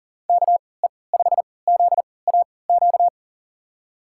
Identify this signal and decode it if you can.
Morse code, heard as a keyed tone, about 40 wpm, spaced as for 23 wpm, 690 Hz. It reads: KE5ZAQ
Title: USA Call Signs - 3x Character Spacing - 40wpm